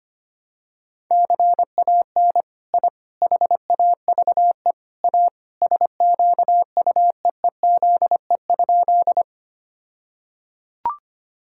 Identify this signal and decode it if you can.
Morse code, heard as a keyed tone, about 25 wpm, 700 Hz. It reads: CAN I HAVE A SQUEEZE?
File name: practice / 25wpm / All-Age Appropriate Jokes - 25wpm